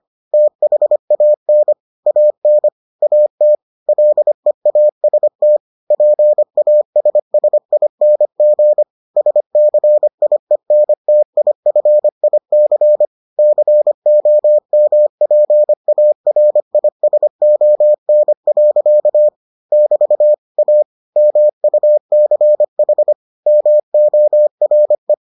Morse code, heard as a keyed tone, about 25 wpm, 600 Hz. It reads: THAN AN AT LEAST PASSING SCIENTIFIC COMPARISON. <BT> A MUCH MORE